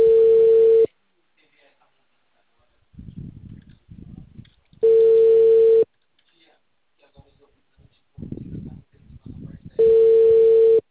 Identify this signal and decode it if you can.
Ringback tone (China, 450 Hz), with speech.